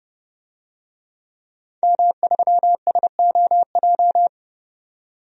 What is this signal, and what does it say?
Morse code, heard as a keyed tone, about 30 wpm, 700 Hz. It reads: M3SOJ
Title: DXCC-100 Call Signs - 5 Characters - 30wpm